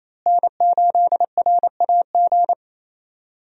Morse code, heard as a keyed tone, about 28 wpm, 700 Hz. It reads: N8RAG
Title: USA Call Signs - 5 Characters - Repeated 2x - 28wpm